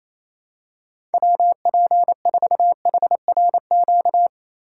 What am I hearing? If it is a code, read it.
Morse code, heard as a keyed tone, about 27 wpm, 700 Hz. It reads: WP4HRQ
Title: USA Call Signs - 2x Character Spacing - 28wpm